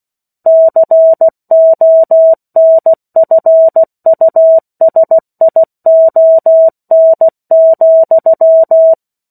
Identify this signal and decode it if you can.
Morse code, heard as a keyed tone, about 16 wpm, 650 Hz. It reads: CONFUSION,
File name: 03-Chapter-03 16 WPM